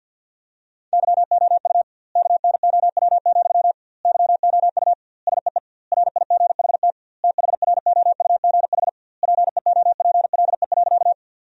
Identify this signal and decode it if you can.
Morse code, heard as a keyed tone, about 50 wpm, 700 Hz. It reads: YOU KNOW, YOU SEE RIGHT THROUGH PEOPLE.
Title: 2x Word-Spacing - Sentences from Top 200 Words - 50wpm